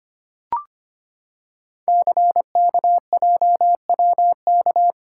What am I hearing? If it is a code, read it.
Morse code, heard as a keyed tone, about 25 wpm, 700 Hz. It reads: CKJWK
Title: Code Groups - 5 Letters - 25wpm